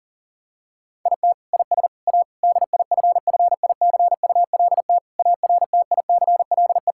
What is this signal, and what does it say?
Morse code, heard as a keyed tone, about 37 wpm, 700 Hz. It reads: IT IS A DIFFICULT ARTICLE
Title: Sentences from Top 1000 Words - Review Part 5 - 40wpm